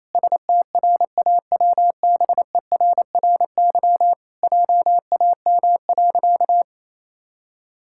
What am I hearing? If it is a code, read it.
Morse code, heard as a keyed tone, about 28 wpm, 700 Hz. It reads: STRAWBERRY JAM.
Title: Pangram Challenge v2 - 28wpm